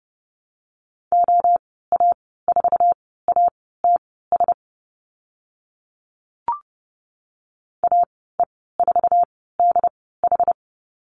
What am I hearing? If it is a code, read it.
Morse code, heard as a keyed tone, about 30 wpm, spaced as for 17 wpm, 700 Hz. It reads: OA4ATS AE4DH
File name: Lesson 14g - Number 2 - Call Signs 30wpm